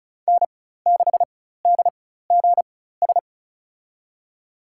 Morse code, heard as a keyed tone, about 34 wpm, spaced as for 16 wpm, 700 Hz. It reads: N6DGS
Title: USA Call Signs - 4x Character Spacing - 35wpm